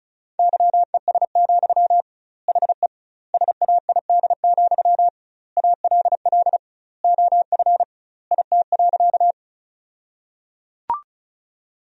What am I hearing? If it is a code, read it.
Morse code, heard as a keyed tone, about 36 wpm, 700 Hz. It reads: YES, HE SAID, ALL OF IT.
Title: 2x Word-Spacing - Sentences from Top 500 Words - 35wpm